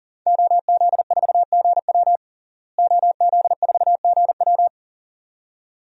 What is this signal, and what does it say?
Morse code, heard as a keyed tone, about 40 wpm, 700 Hz. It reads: OZ4GW OZ4GW
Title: DXCC-100 Call Signs - 5 Characters - Repeated 2x - 40wpm